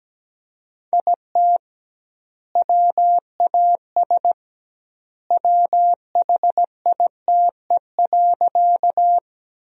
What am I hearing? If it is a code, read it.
Morse code, heard as a keyed tone, about 17 wpm, 700 Hz. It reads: IT WAS WHITE.
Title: 2x Word-Spacing - Sentences from Top 300 Words - 17wpm